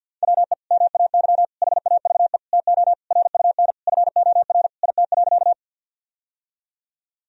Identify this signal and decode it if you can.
Morse code, heard as a keyed tone, about 52 wpm, 700 Hz. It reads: WE MAY HAVE TO RUN FOR IT.
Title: Sentences from Top 1000 Words - Review Part 1 - 50wpm